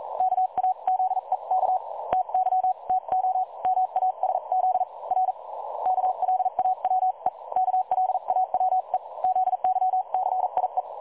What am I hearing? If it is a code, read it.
Morse code, heard as a keyed tone, about 38 wpm, 750 Hz. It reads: KAZEK TY TO MASZ R KLAWE KLAWE ZYCIE
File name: dlazaawansowanych / nagraniazpasma / websdr recording 3565.0kHz